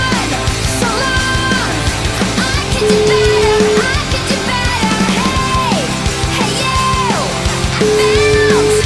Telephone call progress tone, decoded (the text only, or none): ringback tone